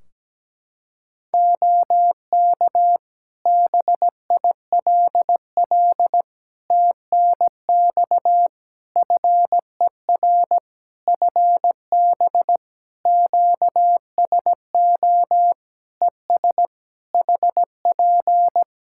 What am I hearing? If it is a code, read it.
Morse code, heard as a keyed tone, about 17 wpm, 700 Hz. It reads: OK BILL TNX FER FB QSO ES HP
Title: Practice QSO - Exchange 4 - Ending - 17wpm